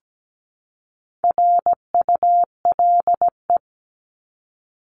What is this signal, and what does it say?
Morse code, heard as a keyed tone, about 17 wpm, 700 Hz. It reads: RULE
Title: ES:S Suffix - Top 100 Words with Base Words 17wpm